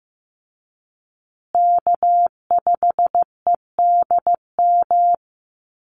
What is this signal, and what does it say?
Morse code, heard as a keyed tone, about 15 wpm, 700 Hz. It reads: K5EDM
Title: Call Signs Repeated 3x - 15wpm